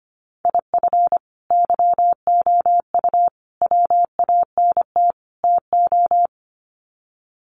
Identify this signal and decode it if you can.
Morse code, heard as a keyed tone, about 25 wpm, 700 Hz. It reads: IF YOU WANT TO
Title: Sets of 4 Words 25wpm